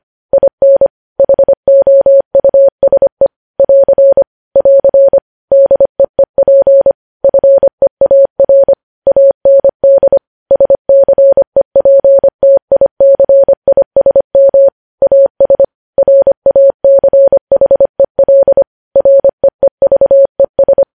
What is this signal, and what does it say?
Morse code, heard as a keyed tone, about 25 wpm, 550 Hz. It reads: IN HOUSE <AR> <AR> DEEP FEAR AND SCEPTICISM AS RACHEL REEVES
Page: News Headlines in Morse Code at 25 WPM